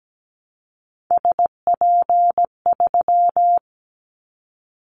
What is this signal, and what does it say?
Morse code, heard as a keyed tone, about 17 wpm, 700 Hz. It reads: SP3
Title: Callsign Prefixes for Top 25 DXCC - Repeated 3x - 17wpm